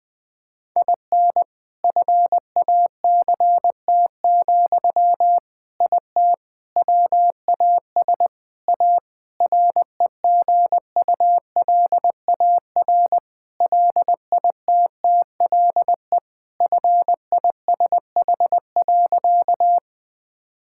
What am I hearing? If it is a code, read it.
Morse code, heard as a keyed tone, about 21 wpm, 700 Hz. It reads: IN FACT, IT WAS A REGULAR LITTLE FISH.
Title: Sentences from Top 1000 Words - Review Part 6 - 20wpm